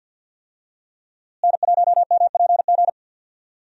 Morse code, heard as a keyed tone, about 52 wpm, 700 Hz. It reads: N1MPG
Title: USA Call Signs - 5 Characters - Repeated 2x - 50wpm